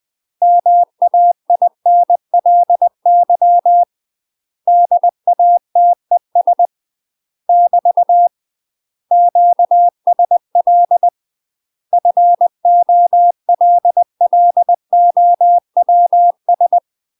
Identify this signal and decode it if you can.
Morse code, heard as a keyed tone, about 20 wpm, 700 Hz. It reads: MAINLY DATES <BT> QSL FOLLOWS
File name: UK MASTER QSOs TRAINING / 20WPM / UK MASTER QSOs 0034